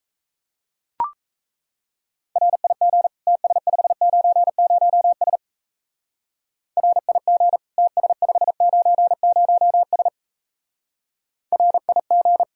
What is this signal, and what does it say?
Morse code, heard as a keyed tone, about 34 wpm, 700 Hz. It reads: RIG TS590S RIG TS590S RIG
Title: Speed-Racing Common QSO Phrases Encore - 28wpm